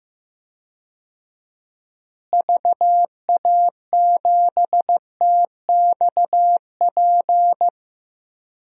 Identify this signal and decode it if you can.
Morse code, heard as a keyed tone, about 15 wpm, 700 Hz. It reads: VA7TXP
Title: DXCC-100 Call Signs - 6 Characters - Repeated 2x - 15wpm